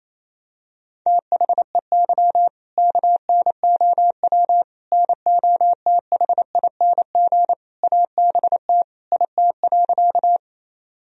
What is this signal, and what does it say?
Morse code, heard as a keyed tone, about 28 wpm, 700 Hz. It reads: THEY KNOW NOTHING ABT IT.
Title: Sentences from Top 500 Words and Abbreviations 28wpm